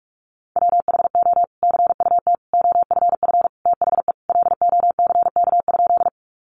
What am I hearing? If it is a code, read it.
Morse code, heard as a keyed tone, about 44 wpm, 700 Hz. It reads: WHO CUT OFF THE LOCK?
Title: Sentences from Top 1300 Words 45wpm